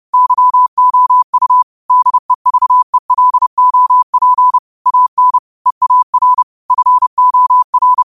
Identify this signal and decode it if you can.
Morse code, heard as a keyed tone, about 30 wpm, 1000 Hz. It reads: YOU DEVELOP AN EAR FOR
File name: qotdfiles / 30wpm